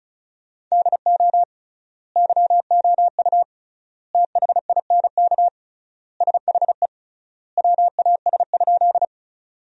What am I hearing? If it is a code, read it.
Morse code, heard as a keyed tone, about 36 wpm, 700 Hz. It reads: DO YOU THINK SHE WAS?
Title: 3x Word-Spacing - Sentences from Top 200 Words - 35wpm